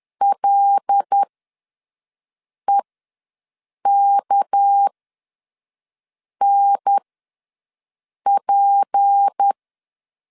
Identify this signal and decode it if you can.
Morse code, heard as a keyed tone, about 10 wpm, spaced as for 5 wpm, 800 Hz. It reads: LEKNP
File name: LEZIONI / 6 PKX0 / 6 PK ws 10wpm 3745